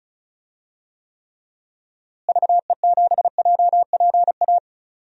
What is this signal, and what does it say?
Morse code, heard as a keyed tone, about 36 wpm, 700 Hz. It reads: VE7JPA
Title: DXCC-100 Call Signs - 6 Characters - Repeated 2x - 35wpm